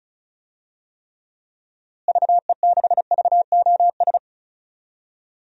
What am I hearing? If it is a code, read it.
Morse code, heard as a keyed tone, about 36 wpm, 700 Hz. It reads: VE6VOS
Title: DXCC-100 Call Signs - 6 Characters - Repeated 2x - 35wpm